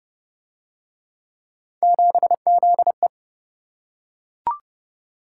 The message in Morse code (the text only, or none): 7ZE